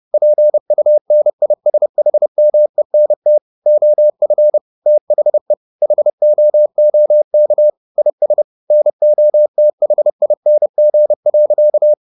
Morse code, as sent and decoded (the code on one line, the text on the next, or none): .--. ..- -. .. ... .... -- . -. - / --- ..-. / - .... / .... --- --- -.- / .. ... / -. --- - .... .. -. --. .-.-.-
PUNISHMENT OF THE HOOK IS NOTHING.